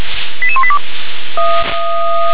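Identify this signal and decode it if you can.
Five-tone selective call (ZVEI), heard as quick runs of five tones, tones of about 70 ms. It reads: 88182
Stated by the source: ZVEI five-tone selective call received on an SDR.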